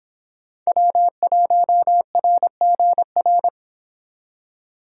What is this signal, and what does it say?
Morse code, heard as a keyed tone, about 30 wpm, 700 Hz. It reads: W1RGR